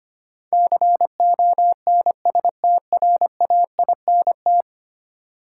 Morse code, heard as a keyed tone, about 25 wpm, 700 Hz. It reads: CONSTRAINT
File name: Ten Letter Words - 25wpm